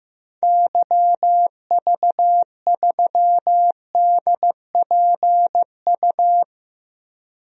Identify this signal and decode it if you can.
Morse code, heard as a keyed tone, about 15 wpm, 700 Hz. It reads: YV3DPU
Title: DXCC-100 Call Signs - 6 Characters - 15wpm